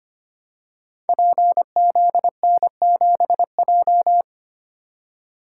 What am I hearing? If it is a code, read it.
Morse code, heard as a keyed tone, about 25 wpm, 700 Hz. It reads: PZN7J
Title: Code Groups - 5 Letter:Numbers - 25wpm